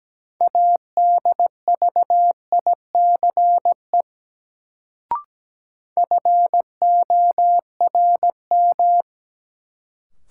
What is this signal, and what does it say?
Morse code, heard as a keyed tone, about 17 wpm, 700 Hz. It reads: ADVICE FORM
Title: Sentences from Top 1100 Words - New Word Drill Alt - 17wpm